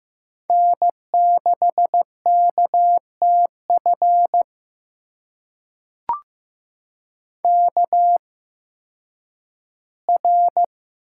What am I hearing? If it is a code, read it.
Morse code, heard as a keyed tone, about 15 wpm, spaced as for 10 wpm, 700 Hz. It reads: N6KTF KR
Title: USA Call Signs - 8x Character Spacing - 15wpm